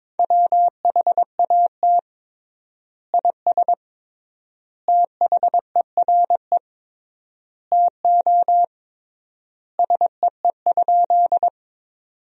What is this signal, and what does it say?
Morse code, heard as a keyed tone, about 22 wpm, 700 Hz. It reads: WHAT IS THERE TO SEE?
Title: 3x Word-Spacing - Sentences from Top 100 Words - 22wpm